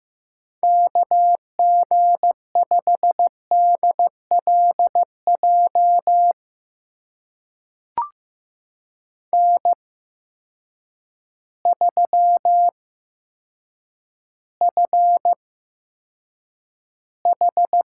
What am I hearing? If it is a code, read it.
Morse code, heard as a keyed tone, about 15 wpm, spaced as for 8 wpm, 700 Hz. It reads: KG5DLJ N3FH
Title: USA Call Signs - 8x Character Spacing - 15wpm